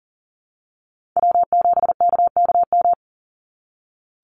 Morse code, heard as a keyed tone, about 40 wpm, 700 Hz. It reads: W7KKM
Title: NA SOTA Chaser Callsigns 40wpm